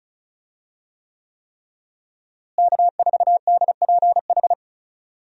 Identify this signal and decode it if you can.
Morse code, heard as a keyed tone, about 36 wpm, 700 Hz. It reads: K4DPH
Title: USA Call Signs - 5 Characters - Repeated 3x - 35wpm